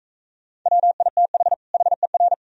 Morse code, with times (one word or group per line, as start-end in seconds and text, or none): 0.7-1.5 WITH
1.7-2.3 HER